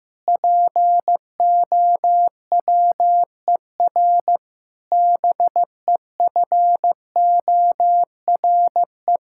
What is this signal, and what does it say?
Morse code, heard as a keyed tone, about 15 wpm, 700 Hz. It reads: POWER BEFORE
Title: Sentences from Top 1000 Words - Review Part 6 - 15wpm